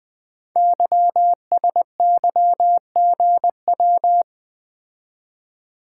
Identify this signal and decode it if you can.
Morse code, heard as a keyed tone, about 20 wpm, 700 Hz. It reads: YSYGW
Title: Code Groups - 5 Letters - 20wpm